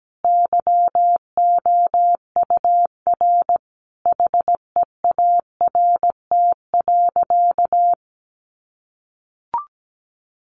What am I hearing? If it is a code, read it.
Morse code, heard as a keyed tone, about 17 wpm, 700 Hz. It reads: YOUR HEART.
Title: Sentences from Top 1400 Words 17wpm